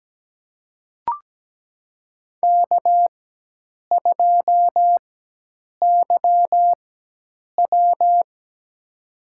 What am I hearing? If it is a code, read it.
Morse code, heard as a keyed tone, about 17 wpm, spaced as for 8 wpm, 700 Hz. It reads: K2YW
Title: USA Call Signs - 4 Characters - 4x Character Spacing - 17wpm